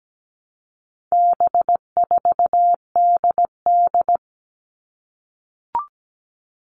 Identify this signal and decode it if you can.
Morse code, heard as a keyed tone, about 17 wpm, 700 Hz. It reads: B4DD
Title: All Cut Numbers - 4-Digit 17wpm